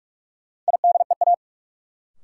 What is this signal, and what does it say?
Morse code, heard as a keyed tone, about 46 wpm, 700 Hz. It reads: IDEA
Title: Sets of 3 Words - New Word Drill - 45wpm